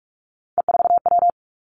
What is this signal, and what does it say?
Morse code, heard as a keyed tone, about 46 wpm, 700 Hz. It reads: E4W
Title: Three Challenging Letter-Number Combinations 45wpm